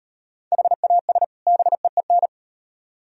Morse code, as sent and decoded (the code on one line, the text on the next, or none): .... .- ... / -... . -.
HAS BEEN